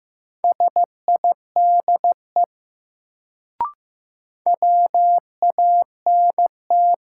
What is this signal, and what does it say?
Morse code, heard as a keyed tone, about 15 wpm, 700 Hz. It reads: SIDE WANT
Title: Sets of 2 Words - New Word Drill - 15wpm